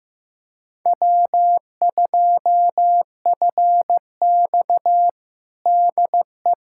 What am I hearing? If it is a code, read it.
Morse code, heard as a keyed tone, about 16 wpm, 700 Hz. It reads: W2FX DE
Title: USA Callsign DE Callsign - 4 Characters - 15wpm